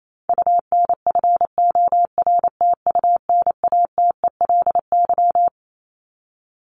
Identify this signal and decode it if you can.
Morse code, heard as a keyed tone, about 28 wpm, 700 Hz. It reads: UNFORTUNATELY
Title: Top 1751-2000 Words 28wpm